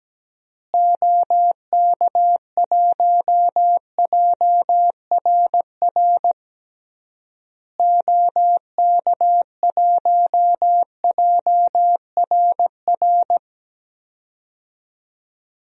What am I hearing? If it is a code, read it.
Morse code, heard as a keyed tone, about 17 wpm, 700 Hz. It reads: OK1JRR OK1JRR
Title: DXCC-100 Call Signs - 6 Characters - Repeated 3x - 17wpm